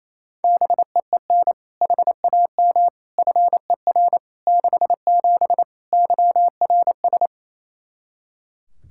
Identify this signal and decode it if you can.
Morse code, heard as a keyed tone, about 28 wpm, 700 Hz. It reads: BEEN HAM FER 67 YRS
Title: Practice QSO - Exchange 3 - More Info - 28wpm